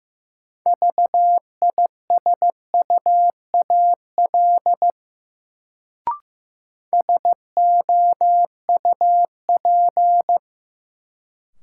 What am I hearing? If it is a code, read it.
Morse code, heard as a keyed tone, about 15 wpm, 700 Hz. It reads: VISUAL SOUP
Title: Sentences from Top 1000 Words - Review Part 6 - New Word Drill - 15wpm